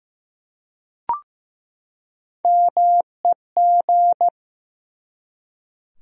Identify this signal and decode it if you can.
Morse code, heard as a keyed tone, about 15 wpm, 700 Hz. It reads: MEG
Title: USA Common Names - Max 4 Letters - 15wpm